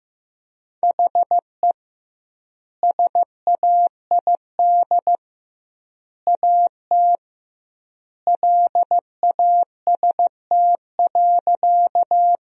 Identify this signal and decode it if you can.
Morse code, heard as a keyed tone, about 16 wpm, 700 Hz. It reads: HE SAID AT LAST.
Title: 2x Word-Spacing - Sentences from Top 300 Words - 15wpm